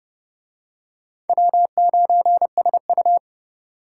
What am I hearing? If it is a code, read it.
Morse code, heard as a keyed tone, about 26 wpm, 700 Hz. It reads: W9SU